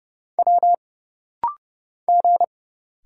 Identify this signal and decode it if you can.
Morse code, heard as a keyed tone, about 30 wpm, 700 Hz. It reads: W G
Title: Single Letters - Rapid-Fire 30wpm